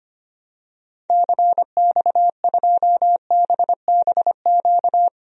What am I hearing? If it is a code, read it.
Morse code, heard as a keyed tone, about 25 wpm, 700 Hz. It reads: CX2BBQ
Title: DXCC-100 Call Signs - 6 Characters - 25wpm